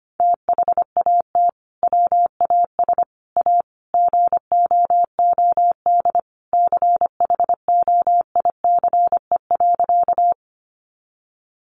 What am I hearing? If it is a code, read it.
Morse code, heard as a keyed tone, about 25 wpm, 700 Hz. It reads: THAT WAS A GOOD CHOICE.
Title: Sentences from Top 900 Words 25wpm